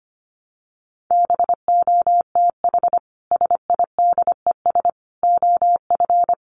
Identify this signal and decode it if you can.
Morse code, heard as a keyed tone, about 25 wpm, 700 Hz. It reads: BOTH SIDES OF
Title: Sets of 4 Words 25wpm